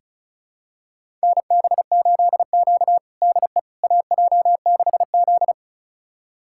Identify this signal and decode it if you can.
Morse code, heard as a keyed tone, about 36 wpm, 700 Hz. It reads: NB8Q DE AJ6Z
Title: USA Callsign DE Callsign - 4 Characters - 35wpm